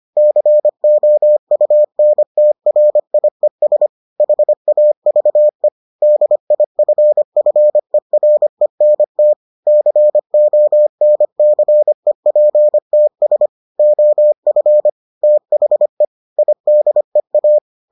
Morse code, heard as a keyed tone, about 25 wpm, 600 Hz. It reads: COUNTRIES HAVE DIFFERENT CONCEPTS OF THE IDEA